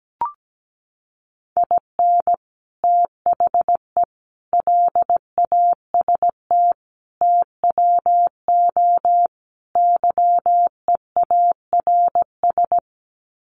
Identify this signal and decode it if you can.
Morse code, heard as a keyed tone, about 17 wpm, 700 Hz. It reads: IN THE LAST TWO YEARS
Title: Sets of 5 Words 17wpm